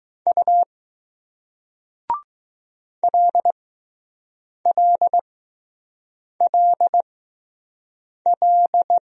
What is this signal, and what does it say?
Morse code, heard as a keyed tone, about 20 wpm, spaced as for 6 wpm, 700 Hz. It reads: U LLLL